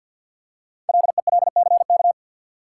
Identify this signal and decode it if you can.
Morse code, heard as a keyed tone, about 48 wpm, 700 Hz. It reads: LELCK
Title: Code Groups - 5 Letters - 50wpm